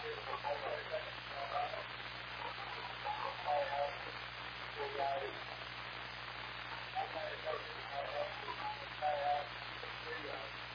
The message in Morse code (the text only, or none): none